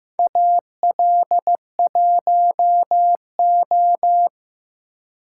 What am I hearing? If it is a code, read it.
Morse code, heard as a keyed tone, about 15 wpm, 700 Hz. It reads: AL1O
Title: USA Call Signs - 4 Characters - 3x Character Spacing - 15wpm